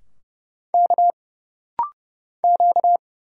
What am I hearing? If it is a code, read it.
Morse code, heard as a keyed tone, about 31 wpm, 700 Hz. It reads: K Q
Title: Single Letters - Rapid-Fire 30wpm